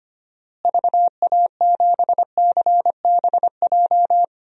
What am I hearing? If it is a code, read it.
Morse code, heard as a keyed tone, about 25 wpm, 700 Hz. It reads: VA7CBJ
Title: DXCC-100 Call Signs - 6 Characters - Repeated 3x - 25wpm